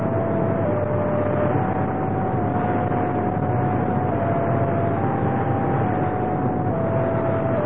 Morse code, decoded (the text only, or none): none